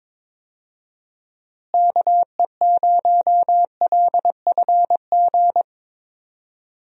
Morse code, heard as a keyed tone, about 22 wpm, 700 Hz. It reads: KE0LFG
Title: USA Call Signs - 6 Characters - Repeated 2x - 22wpm